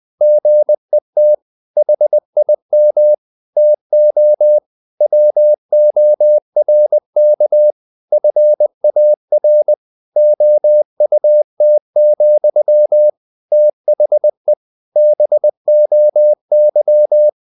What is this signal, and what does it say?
Morse code, heard as a keyed tone, about 20 wpm, 600 Hz. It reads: GET HIM TO WORK FAR OUT, THE BOY
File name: the old man and the sea / the old man and the sea part 1 20wpm